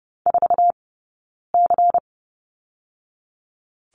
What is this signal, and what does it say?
Morse code, heard as a keyed tone, about 30 wpm, 700 Hz. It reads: 4 C
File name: 092 Lesson 16d - Letter c - Two Characters 30wpm